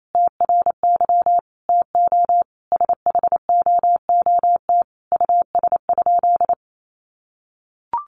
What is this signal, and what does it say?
Morse code, heard as a keyed tone, about 28 wpm, 700 Hz. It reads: TRY TO SHOOT US?
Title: Sentences from Top 1200 Words 28wpm